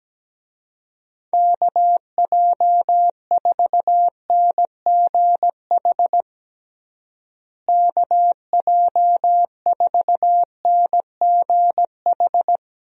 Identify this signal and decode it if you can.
Morse code, heard as a keyed tone, about 17 wpm, 700 Hz. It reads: KJ4NGH KJ4NGH
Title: USA Call Signs - 6 Characters - Repeated 2x - 17wpm